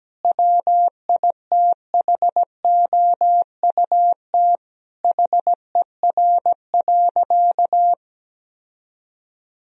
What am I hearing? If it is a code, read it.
Morse code, heard as a keyed tone, about 17 wpm, 700 Hz. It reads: WITHOUT HER.
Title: Sentences from Top 1000 Words - Review Part 3 - 17wpm